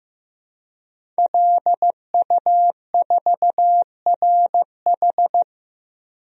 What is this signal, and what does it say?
Morse code, heard as a keyed tone, about 15 wpm, 700 Hz. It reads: LU4RH
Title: DXCC-100 Call Signs - 5 Characters - Repeated 2x - 15wpm